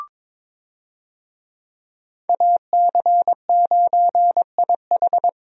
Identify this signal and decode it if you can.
Morse code, heard as a keyed tone, about 22 wpm, 700 Hz. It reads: AC9IH
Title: USA Call Signs - 5 Characters - 22wpm